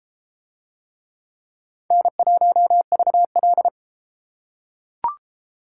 Morse code, heard as a keyed tone, about 30 wpm, 700 Hz. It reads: N1VL